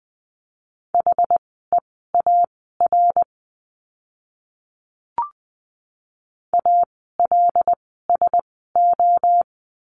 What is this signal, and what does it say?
Morse code, heard as a keyed tone, about 20 wpm, spaced as for 15 wpm, 700 Hz. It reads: HEAR ALSO
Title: 115 Lesson 19f - Letter w - Words 20wpm